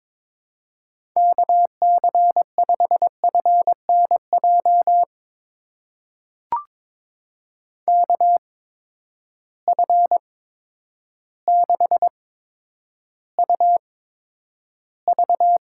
Morse code, heard as a keyed tone, about 22 wpm, spaced as for 10 wpm, 700 Hz. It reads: KC5FNJ KF6UV
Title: USA Call Signs - 6 Characters - 8x Character Spacing - 22wpm